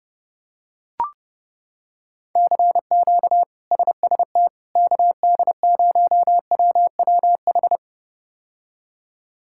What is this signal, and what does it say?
Morse code, heard as a keyed tone, about 30 wpm, 700 Hz. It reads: CQ SST KD0WWH
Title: K1USN SST QSO Practice - 30wpm